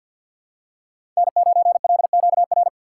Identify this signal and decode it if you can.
Morse code, heard as a keyed tone, about 51 wpm, 700 Hz. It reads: N9LQR
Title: USA Call Signs - 5 Characters - 2x Character Spacing - 50wpm